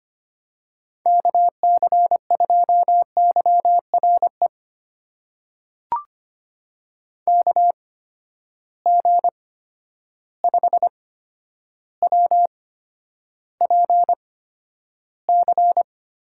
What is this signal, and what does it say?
Morse code, heard as a keyed tone, about 25 wpm, spaced as for 11 wpm, 700 Hz. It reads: KC2YRE KG5WPC
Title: USA Call Signs - 6 Characters - 8x Character Spacing - 25wpm